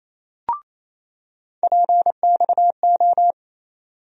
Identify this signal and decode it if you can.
Morse code, heard as a keyed tone, about 28 wpm, 700 Hz. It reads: PXO
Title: USA Call Sign Postfixes - 3 Letters 28wpm